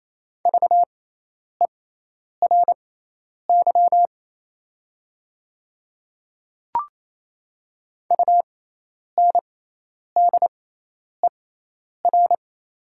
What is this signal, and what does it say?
Morse code, heard as a keyed tone, about 28 wpm, spaced as for 10 wpm, 700 Hz. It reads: VERY UNDER
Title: 6x Character Spacing - Top 200 Words - 28wpm